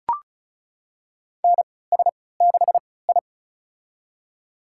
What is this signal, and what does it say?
Morse code, heard as a keyed tone, about 36 wpm, spaced as for 20 wpm, 700 Hz. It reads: NS6I
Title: USA Call Signs - 4 Characters - 3x Character Spacing - 35wpm